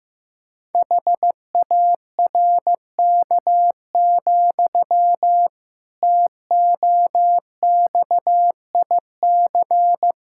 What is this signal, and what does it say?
Morse code, heard as a keyed tone, about 15 wpm, 700 Hz. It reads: HARK, TOXIC
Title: Pangram Challenge v2 - 15wpm